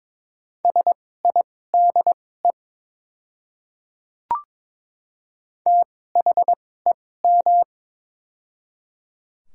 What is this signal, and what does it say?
Morse code, heard as a keyed tone, about 21 wpm, spaced as for 16 wpm, 700 Hz. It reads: SIDE THEM
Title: 2x Character Spacing - Top 100 Words - 22wpm